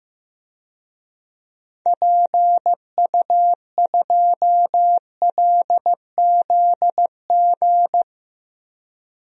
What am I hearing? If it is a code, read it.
Morse code, heard as a keyed tone, about 15 wpm, 700 Hz. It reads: PU2LZG
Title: DXCC-100 Call Signs - 6 Characters - 15wpm